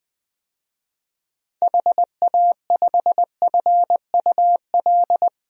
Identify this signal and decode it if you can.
Morse code, heard as a keyed tone, about 20 wpm, 700 Hz. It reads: HA5FUL
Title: DXCC-100 Call Signs - 6 Characters - 20wpm